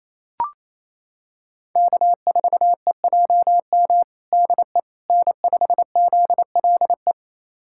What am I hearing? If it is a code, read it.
Morse code, heard as a keyed tone, about 28 wpm, 700 Hz. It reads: K4EJM DE N5ZLE
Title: USA Callsign DE Callsign - 5 Characters - 28wpm